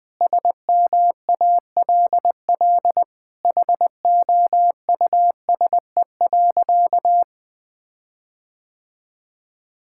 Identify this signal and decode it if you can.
Morse code, heard as a keyed tone, about 20 wpm, 700 Hz. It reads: SMALL HOUSE.
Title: Sentences with 8-Word Phrases - Spoken Beginning - 20wpm